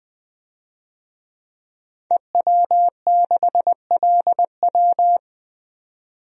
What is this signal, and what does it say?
Morse code, heard as a keyed tone, about 21 wpm, 700 Hz. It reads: EW6LW
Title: DXCC-100 Call Signs - 5 Characters - 20wpm